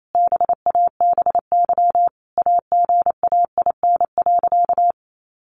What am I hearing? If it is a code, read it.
Morse code, heard as a keyed tone, about 28 wpm, 700 Hz. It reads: BABY AGAIN.
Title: Sentences from Top 1300 Words 28wpm